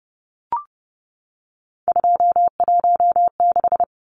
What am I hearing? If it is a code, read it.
Morse code, heard as a keyed tone, about 30 wpm, 700 Hz. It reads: 216
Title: Numbers - 3-Digit 30wpm